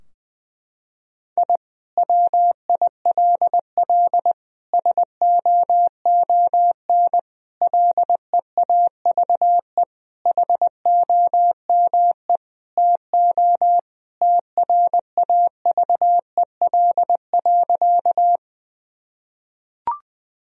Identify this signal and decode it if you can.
Morse code, heard as a keyed tone, about 21 wpm, 700 Hz. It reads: I WILL SOON LEAVE HOME TO TRAVEL.
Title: Sentences from Top 1000 Words - Review Part 2 - 20wpm